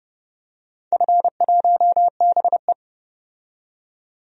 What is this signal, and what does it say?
Morse code, heard as a keyed tone, about 31 wpm, 700 Hz. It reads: F1BE
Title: DXCC-100 Call Signs - 4 Characters - 30wpm